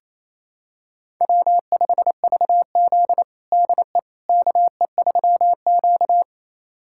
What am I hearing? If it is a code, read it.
Morse code, heard as a keyed tone, about 28 wpm, 700 Hz. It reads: W5VZ DE KE3Q
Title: USA Callsign DE Callsign - 4 Characters - 28wpm